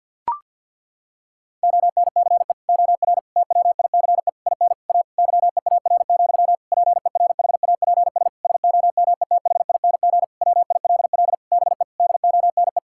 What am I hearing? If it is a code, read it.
Morse code, heard as a keyed tone, about 46 wpm, 700 Hz. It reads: ONCE OR TWICE IN A YEAR, PERHAPS SOMETHING WILL BE DONE
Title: Sentences from Top 1600 Words 50wpm